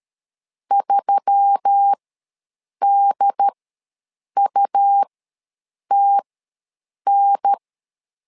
Morse code, heard as a keyed tone, about 13 wpm, spaced as for 7 wpm, 800 Hz. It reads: 3DUTN
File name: LEZIONI / 4 Q1S3 ws 12wpm 2038